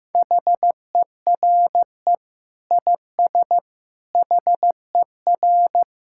Morse code, heard as a keyed tone, about 15 wpm, 700 Hz. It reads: HERE IS HER
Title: Sentences from Top 1600 Words 15wpm